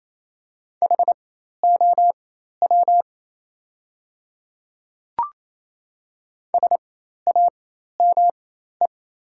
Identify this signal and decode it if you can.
Morse code, heard as a keyed tone, about 28 wpm, spaced as for 13 wpm, 690 Hz. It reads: HOW SAME
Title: 4x Character Spacing - Top 200 Words - 28wpm